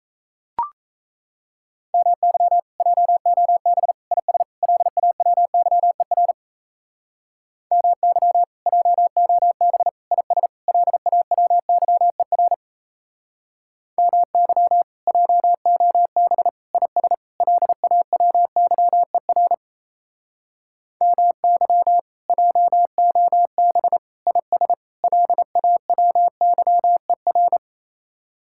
Morse code, heard as a keyed tone, about 33 wpm, 700 Hz. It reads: MY JOB IS LAWYER MY JOB IS LAWYER MY JOB IS LAWYER MY JOB IS LAWYER
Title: Speed-Racing Common QSO Phrases - 28wpm